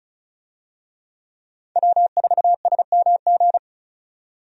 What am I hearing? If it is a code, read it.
Morse code, heard as a keyed tone, about 36 wpm, 700 Hz. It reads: W4SMG
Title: USA Call Signs - 5 Characters - 35wpm